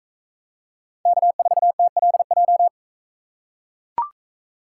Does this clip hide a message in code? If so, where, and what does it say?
Yes: Morse code from 1.1 to 2.7 s, heard as a keyed tone, about 43 wpm, 700 Hz. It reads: K4TLJ